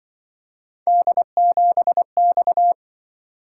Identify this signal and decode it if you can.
Morse code, heard as a keyed tone, about 24 wpm, 700 Hz. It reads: D7X
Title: Speed-Racing - Three Letter-Number Combinations - 20wpm